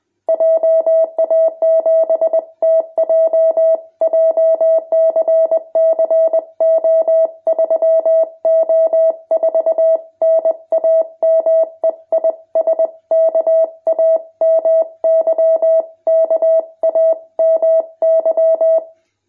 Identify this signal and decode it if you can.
Morse code, heard as a keyed tone, about 20 wpm, 650 Hz. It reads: JA7TJ JCC O3O4 NAME IS KAMY KAMY
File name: jh8cbh / 2021cw